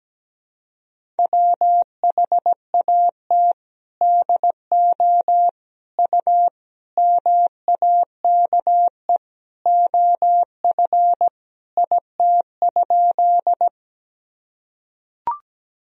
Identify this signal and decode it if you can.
Morse code, heard as a keyed tone, about 17 wpm, 700 Hz. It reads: WHAT DO U MAKE OF IT?
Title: Sentences from Top 100 Words and Abbreviations 17wpm